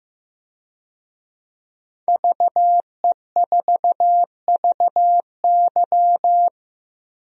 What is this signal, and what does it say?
Morse code, heard as a keyed tone, about 15 wpm, 700 Hz. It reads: VE4VY
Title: DXCC-100 Call Signs - 5 Characters - 15wpm